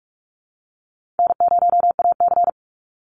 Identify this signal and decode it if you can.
Morse code, heard as a keyed tone, about 45 wpm, 700 Hz. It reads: N0AC
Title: Call Signs Repeated 3x - 45wpm